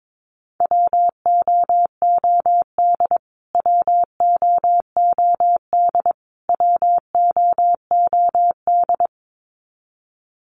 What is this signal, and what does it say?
Morse code, heard as a keyed tone, about 22 wpm, 700 Hz. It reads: WOOD WOOD WOOD
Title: Top 500 Words Repeated 22wpm